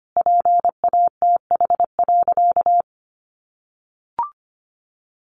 Morse code, heard as a keyed tone, about 25 wpm, 700 Hz. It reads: PATH.
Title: A Sound Of Thunder 25wpm